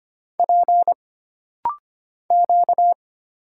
Morse code, heard as a keyed tone, about 25 wpm, 700 Hz. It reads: P Q
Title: Single Letter-Number Alt - Rapid-Fire 25wpm